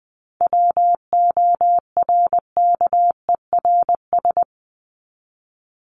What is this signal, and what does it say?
Morse code, heard as a keyed tone, about 20 wpm, 700 Hz. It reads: WORKERS
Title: ES:S Suffix - Top 100 Words with Base Words 20wpm